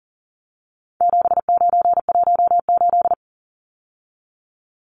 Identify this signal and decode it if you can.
Morse code, heard as a keyed tone, about 40 wpm, 700 Hz. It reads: 7918
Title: Numbers - 4-Digit 40wpm